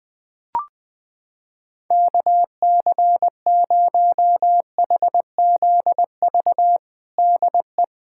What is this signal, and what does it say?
Morse code, heard as a keyed tone, about 20 wpm, 700 Hz. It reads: KC0HZV DE
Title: USA Callsign DE Callsign - 6 Characters - 20wpm